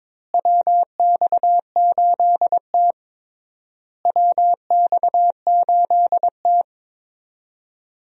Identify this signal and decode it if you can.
Morse code, heard as a keyed tone, about 22 wpm, 700 Hz. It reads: WX8T WX8T
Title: USA Call Signs - 4 Characters - Repeated 3x - 22wpm